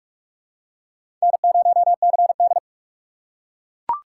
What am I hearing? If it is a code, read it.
Morse code, heard as a keyed tone, about 41 wpm, 700 Hz. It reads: N0CD
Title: USA Call Signs - 4 Characters - 6x Character Spacing - 45wpm